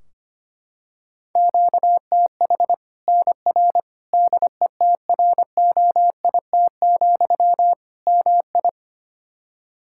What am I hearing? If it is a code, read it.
Morse code, heard as a keyed tone, about 25 wpm, 700 Hz. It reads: QTH NR DETROIT, MI
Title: K1USN SST QSO Practice 25wpm